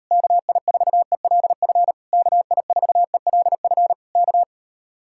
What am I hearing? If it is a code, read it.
Morse code, heard as a keyed tone, about 41 wpm, 700 Hz. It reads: KI4ELF KI4ELF K